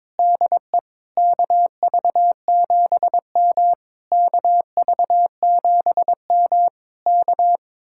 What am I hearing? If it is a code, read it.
Morse code, heard as a keyed tone, about 22 wpm, 700 Hz. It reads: DE KV7M KV7M K
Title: Speed-Racing Common QSO Phrases Encore - 22wpm